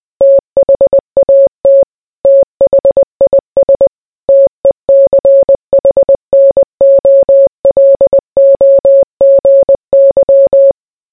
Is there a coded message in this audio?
Yes: Morse code from 0.2 to 10.7 s, heard as a keyed tone, about 20 wpm, 550 Hz. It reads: THAT THIS TECHNOLOGY